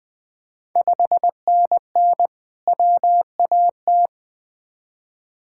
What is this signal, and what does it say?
Morse code, heard as a keyed tone, about 20 wpm, 700 Hz. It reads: 5NN WAT
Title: Ontario QSO Party Exchange 20wpm